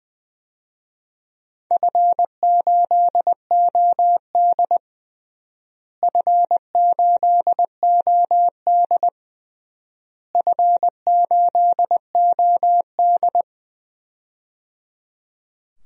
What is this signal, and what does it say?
Morse code, heard as a keyed tone, about 20 wpm, 700 Hz. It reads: F8OD F8OD F8OD
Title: DXCC-100 Call Signs - 4 Characters - Repeated 3x - 20wpm